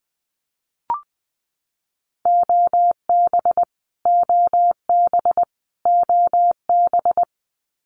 Morse code, heard as a keyed tone, about 20 wpm, 700 Hz. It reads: OB OB OB
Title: Abbreviations Repeated 3x - 20wpm